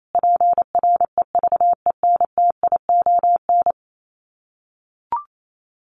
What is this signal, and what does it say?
Morse code, heard as a keyed tone, about 28 wpm, 700 Hz. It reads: PREVENTION
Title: Sets of 5 Words Encore2 28wpm